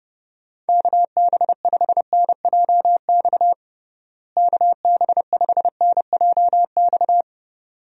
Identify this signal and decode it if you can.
Morse code, heard as a keyed tone, about 32 wpm, 700 Hz. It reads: KB5NJX KB5NJX
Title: USA Call Signs - 6 Characters - Repeated 2x - 30wpm